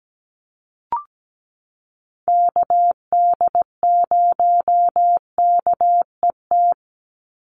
Morse code, heard as a keyed tone, about 17 wpm, 700 Hz. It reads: KD0KET
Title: USA Call Signs 17wpm